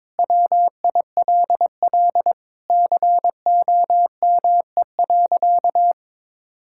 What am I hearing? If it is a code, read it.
Morse code, heard as a keyed tone, about 21 wpm, 700 Hz. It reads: WILL COME.
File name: Sentences with 7-Word Phrases - 22wpm